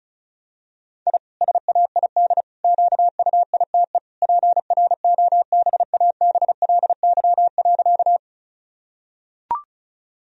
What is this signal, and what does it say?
Morse code, heard as a keyed tone, about 36 wpm, 700 Hz. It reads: I SAID QUITE PROBABLY.
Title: Sentences from Top 1000 Words - Review Part 4 - 35wpm